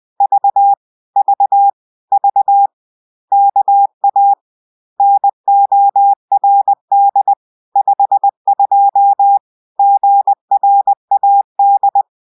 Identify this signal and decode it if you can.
Morse code, heard as a keyed tone, about 20 wpm, 800 Hz. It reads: V V V KA NORD 52 GRAD